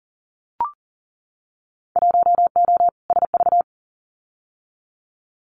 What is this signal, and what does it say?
Morse code, heard as a keyed tone, about 39 wpm, 700 Hz. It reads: 1O SV
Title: Winter Field Day Exchanges 40wpm